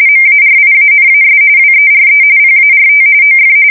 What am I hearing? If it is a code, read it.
RTTY, about 45.45 baud. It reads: MARIANAS WERE PART OF
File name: fsk170 / battleshipnj